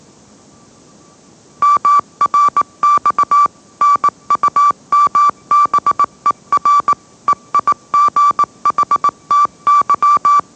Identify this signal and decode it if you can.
Morse code, heard as a keyed tone, about 18 wpm, 1160 Hz. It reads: MRX NUMBER EIGHTY
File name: mrx / mrx082 20wpm